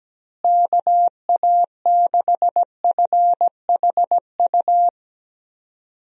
Morse code, heard as a keyed tone, about 17 wpm, 700 Hz. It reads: KA6FHU
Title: USA Call Signs - 6 Characters - Repeated 3x - 17wpm